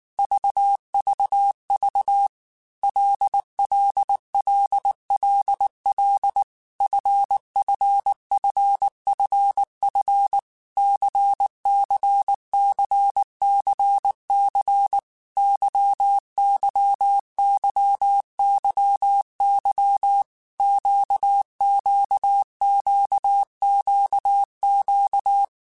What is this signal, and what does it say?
Morse code, heard as a keyed tone, about 19 wpm, 800 Hz. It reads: VVV LLLLL FFFFF CCCCC YYYYY QQQQQ
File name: Groups@18 / Groups@18 mp3 / LFCYQ@18